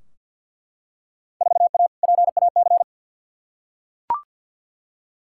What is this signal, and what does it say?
Morse code, heard as a keyed tone, about 50 wpm, 700 Hz. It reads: 4A PAC
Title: Field Day Exchanges Encore - 50wpm